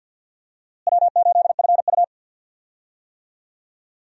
Morse code, heard as a keyed tone, about 52 wpm, 700 Hz. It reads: W8FU
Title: USA Call Signs - 4 Characters - Repeated 2x - 50wpm